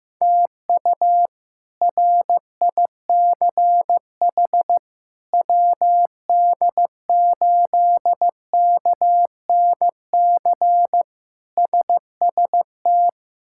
Morse code, heard as a keyed tone, about 15 wpm, 700 Hz. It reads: TU RICH WD8KNC SST
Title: K1USN SST QSO Practice - 15wpm